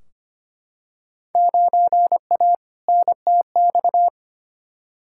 Morse code, heard as a keyed tone, about 25 wpm, 700 Hz. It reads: 9A NTX
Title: Field Day Exchanges Encore - 25wpm